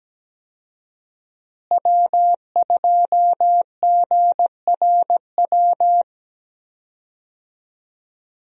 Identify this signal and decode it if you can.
Morse code, heard as a keyed tone, about 17 wpm, 700 Hz. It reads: W2GRW
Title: USA Call Signs - 5 Characters - Repeated 3x - 17wpm